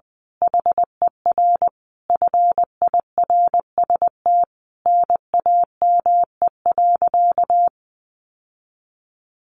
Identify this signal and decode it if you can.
Morse code, heard as a keyed tone, about 20 wpm, 700 Hz. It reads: HER FIRST NAME.
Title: Sentences from Top 1200 Words 20wpm